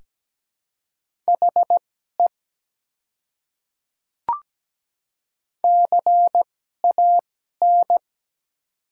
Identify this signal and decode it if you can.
Morse code, heard as a keyed tone, about 17 wpm, spaced as for 12 wpm, 700 Hz. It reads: HE CAN
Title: 2x Character Spacing - Top 200 Words - 17wpm